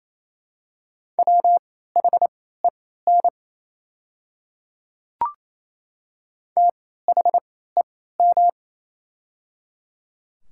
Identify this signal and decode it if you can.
Morse code, heard as a keyed tone, about 28 wpm, spaced as for 16 wpm, 700 Hz. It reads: WHEN THEM
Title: 3x Character Spacing - Top 100 Words - 28wpm